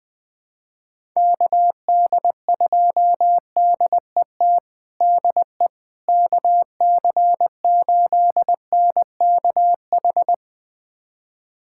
Morse code, heard as a keyed tone, about 21 wpm, 700 Hz. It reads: KD2DET DE KC8NKH
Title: USA Callsign DE Callsign - 6 Characters - 20wpm